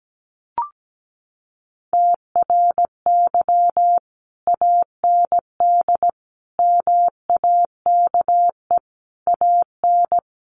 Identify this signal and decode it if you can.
Morse code, heard as a keyed tone, about 17 wpm, 700 Hz. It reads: TRY AND MAKE AN
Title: Sentences from Top 1200 Words 17wpm